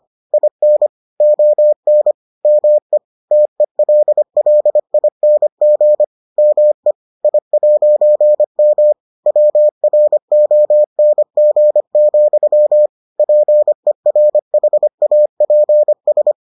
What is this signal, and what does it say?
Morse code, heard as a keyed tone, about 25 wpm, 600 Hz. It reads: IN ON ME TELLING ME I'M WRONG, PERHAPS